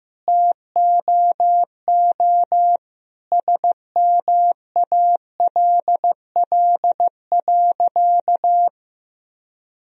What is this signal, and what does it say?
Morse code, heard as a keyed tone, about 15 wpm, 700 Hz. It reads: TOO SMALL.
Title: Sentences from Top 1000 Words - Review Part 1 - 15wpm